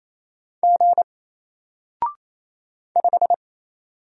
Morse code, heard as a keyed tone, about 28 wpm, 700 Hz. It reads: G 5
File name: Single-Letter-Number-28wpm